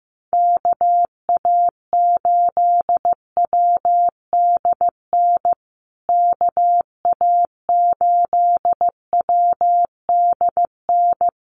Morse code, heard as a keyed tone, about 15 wpm, 700 Hz. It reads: KA8WDN KA8WDN
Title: USA Call Signs Repeated 3x 15wpm